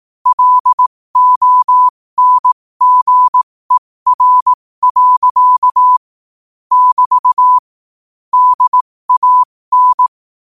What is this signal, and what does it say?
Morse code, heard as a keyed tone, about 18 wpm, spaced as for 15 wpm, 1000 Hz. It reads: LONGER. <BT> DAN
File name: qotdfiles / 15wpm